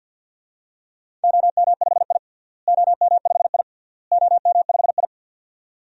Morse code, heard as a keyed tone, about 51 wpm, 700 Hz. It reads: OM5I OM5I OM5I
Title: DXCC-100 Call Signs - 4 Characters - Repeated 3x - 50wpm